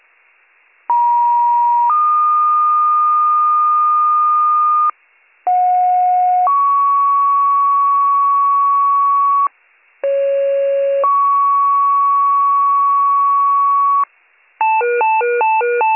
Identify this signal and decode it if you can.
Two-tone page, heard as a one-second tone, then a three-second tone, about 950 then 1220 Hz, about 730 then 1120 Hz, about 550 then 1120 Hz. It is heard from 0.9 to 14.0 s.